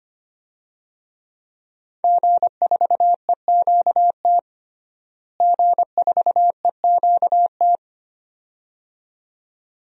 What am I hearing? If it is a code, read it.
Morse code, heard as a keyed tone, about 25 wpm, 700 Hz. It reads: G4EQT G4EQT
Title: DXCC-100 Call Signs - 5 Characters - Repeated 2x - 25wpm